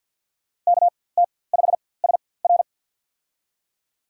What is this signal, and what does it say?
Morse code, heard as a keyed tone, about 47 wpm, spaced as for 23 wpm, 700 Hz. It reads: KT5SR
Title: USA Call Signs - 5 Characters - 4x Character Spacing - 50wpm